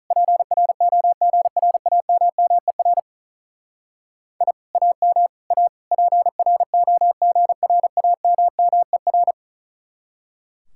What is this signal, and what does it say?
Morse code, heard as a keyed tone, about 43 wpm, 700 Hz. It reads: PROGRAMMER I AM A PROGRAMMER
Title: Speed-Racing Common QSO Phrases - 35wpm